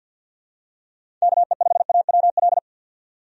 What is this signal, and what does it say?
Morse code, heard as a keyed tone, about 52 wpm, 700 Hz. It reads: KE5AWL
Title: USA Call Signs - 6 Characters - 50wpm